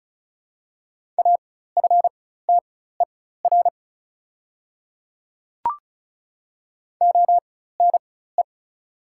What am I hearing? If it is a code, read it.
Morse code, heard as a keyed tone, about 35 wpm, spaced as for 16 wpm, 690 Hz. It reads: AFTER ONE
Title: 4x Character Spacing - Top 200 Words - 35wpm